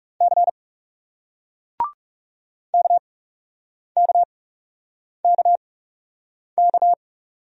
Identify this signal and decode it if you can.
Morse code, heard as a keyed tone, about 40 wpm, spaced as for 8 wpm, 700 Hz. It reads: C KKKK